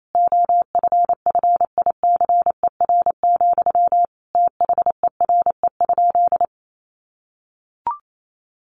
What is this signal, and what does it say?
Morse code, heard as a keyed tone, about 28 wpm, 700 Hz. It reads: OFFICER, THERE?
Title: Sentences from Top 1200 Words 28wpm